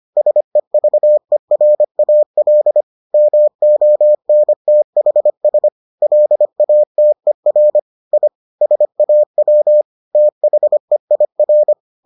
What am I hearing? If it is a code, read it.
Morse code, heard as a keyed tone, about 25 wpm, 600 Hz. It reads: SEVERAL MONTHS LATER I SAW THEIR